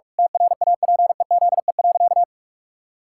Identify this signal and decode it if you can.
Morse code, heard as a keyed tone, about 41 wpm, 700 Hz. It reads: TRAPEZE.
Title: Pangram Challenge v2 - 45wpm